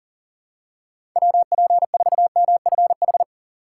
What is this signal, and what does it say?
Morse code, heard as a keyed tone, about 40 wpm, 700 Hz. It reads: WP4MFH
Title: USA Call Signs - 3x Character Spacing - 40wpm